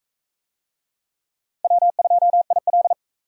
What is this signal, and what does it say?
Morse code, heard as a keyed tone, about 38 wpm, 700 Hz. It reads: W2IL